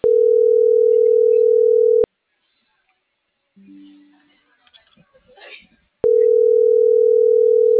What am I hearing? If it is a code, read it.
Ringback tone (North America), 440 + 480 Hz.